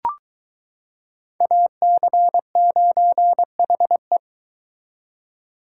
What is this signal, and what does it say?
Morse code, heard as a keyed tone, about 22 wpm, 700 Hz. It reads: AC9HE